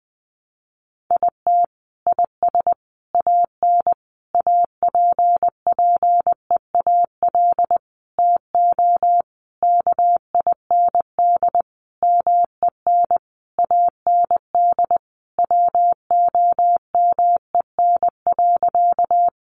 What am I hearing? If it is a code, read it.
Morse code, heard as a keyed tone, about 20 wpm, 700 Hz. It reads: IT IS AN APPEAL TO KIND MEN AND WOMEN.
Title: Sentences from Top 1300 Words 20wpm